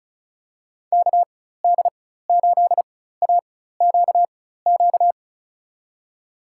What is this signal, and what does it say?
Morse code, heard as a keyed tone, about 36 wpm, spaced as for 16 wpm, 700 Hz. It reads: KD8AQQ
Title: USA Call Signs - 6 Characters - 4x Character Spacing - 35wpm